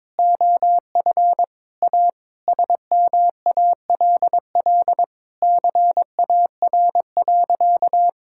Morse code, heard as a keyed tone, about 22 wpm, 700 Hz. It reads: OF A SMALL CAR.
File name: practice / 22wpm / Sentences with 4-Word Phrases - 22wpm